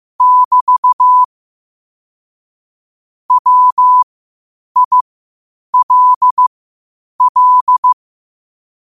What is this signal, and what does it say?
Morse code, heard as a keyed tone, about 15 wpm, spaced as for 8 wpm, 1000 Hz. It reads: <BT> WILL
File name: qotdfiles / 8wpm